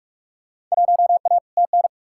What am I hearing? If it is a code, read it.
Morse code, heard as a keyed tone, about 47 wpm, 700 Hz. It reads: 1A TN
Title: Field Day Exchanges - 45wpm